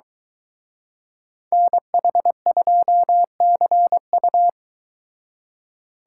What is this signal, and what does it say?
Morse code, heard as a keyed tone, about 23 wpm, 700 Hz. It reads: NH2CU